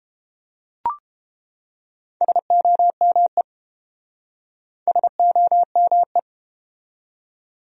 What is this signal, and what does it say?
Morse code, heard as a keyed tone, about 29 wpm, 700 Hz. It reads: SOME SOME